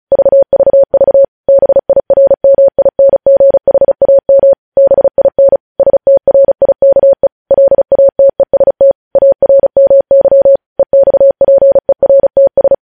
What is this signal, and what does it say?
Morse code, heard as a keyed tone, about 36 wpm, 550 Hz. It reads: VVV BIRMINGHAM BIN STRIKE LATEST ARMY EXPERTS
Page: news headlines in morse code at 35 wpm